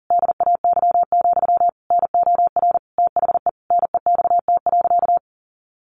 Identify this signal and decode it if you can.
Morse code, heard as a keyed tone, about 40 wpm, 700 Hz. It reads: DAY, NOR THE NEXT.
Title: Sentences from Top 1500 Words 40wpm